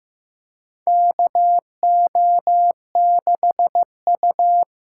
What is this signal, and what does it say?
Morse code, heard as a keyed tone, about 15 wpm, 700 Hz. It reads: KO6U
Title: USA Call Signs - 5 Characters - 6x Character Spacing - 15wpm